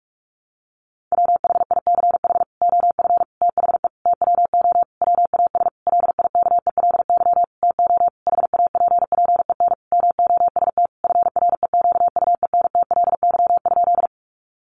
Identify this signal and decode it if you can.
Morse code, heard as a keyed tone, about 44 wpm, 700 Hz. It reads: WHICH OF THE TWO WAS LIKELY TO HAPPEN MOST FREQUENTLY?
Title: Sentences from Top 1300 Words 45wpm